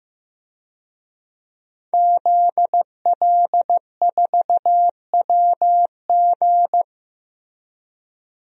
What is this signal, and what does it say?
Morse code, heard as a keyed tone, about 15 wpm, 700 Hz. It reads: ZL4WG
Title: DXCC-100 Call Signs - 5 Characters - Repeated 3x - 15wpm